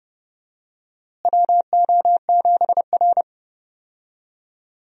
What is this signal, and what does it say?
Morse code, heard as a keyed tone, about 31 wpm, 700 Hz. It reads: WO7R
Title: USA Call Signs - 4 Characters - Repeated 3x - 30wpm